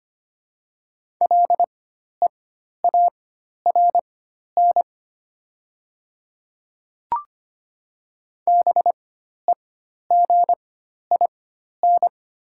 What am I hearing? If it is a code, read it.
Morse code, heard as a keyed tone, about 25 wpm, spaced as for 12 wpm, 700 Hz. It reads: LEARN BEGIN
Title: 4x Character Spacing - Top 300 Words - 25wpm